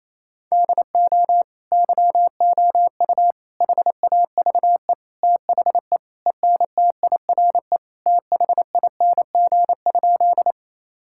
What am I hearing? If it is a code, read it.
Morse code, heard as a keyed tone, about 28 wpm, 700 Hz. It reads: DO YOU HAVE THE ENTIRE THING?
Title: Sentences from Top 1000 Words - Review Part 5 - 28wpm